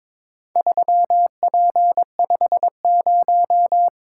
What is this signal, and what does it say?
Morse code, heard as a keyed tone, about 22 wpm, 690 Hz. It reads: 3P50
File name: practice / 22wpm / Four Letter-Number Combinations - 22wpm